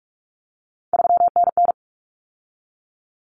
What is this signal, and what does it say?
Morse code, heard as a keyed tone, about 44 wpm, 700 Hz. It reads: 3NN
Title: Common Cut Numbers - 3-Digit 45wpm